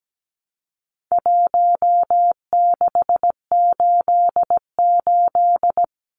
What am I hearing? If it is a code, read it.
Morse code, heard as a keyed tone, about 17 wpm, 700 Hz. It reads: 1688
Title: Numbers - Contest Style - 4-Digit 17wpm